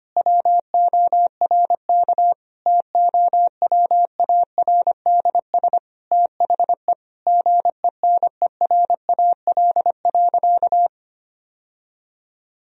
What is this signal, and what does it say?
Morse code, heard as a keyed tone, about 25 wpm, 700 Hz. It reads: WORK TOWARDS THE GENERAL.
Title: Sentences from Top 1000 Words - Review Part 3 - 25wpm